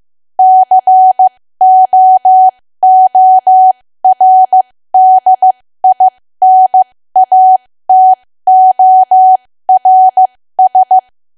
Morse code, heard as a keyed tone, about 15 wpm, spaced as for 13 wpm, 750 Hz. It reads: COORDINATORS
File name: cwa / W1AW-Level2 / 13WPM